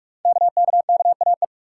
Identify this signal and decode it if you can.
Morse code, heard as a keyed tone, about 44 wpm, 700 Hz. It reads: KKKAE
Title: Code Groups - 5 Letters - 45wpm